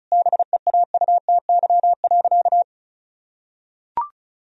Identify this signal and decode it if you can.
Morse code, heard as a keyed tone, about 36 wpm, 700 Hz. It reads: BEAUTY.
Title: Sentences from Top 1000 Words - Review Part 2 - 35wpm